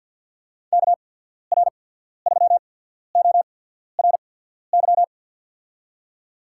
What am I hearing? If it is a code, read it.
Morse code, heard as a keyed tone, about 53 wpm, spaced as for 14 wpm, 700 Hz. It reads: KR3ORY
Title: USA Call Signs - 6 Characters - 8x Character Spacing - 50wpm